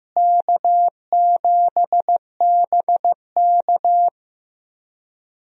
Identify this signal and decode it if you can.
Morse code, heard as a keyed tone, about 15 wpm, 700 Hz. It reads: K7BK
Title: USA Call Signs - 4 Characters - Repeated 3x - 15wpm